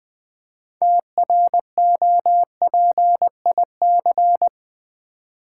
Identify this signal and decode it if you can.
Morse code, heard as a keyed tone, about 20 wpm, 700 Hz. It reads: TROPIC
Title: Build A Word - 3 Letters At A Time - 8-Letter Words - 20wpm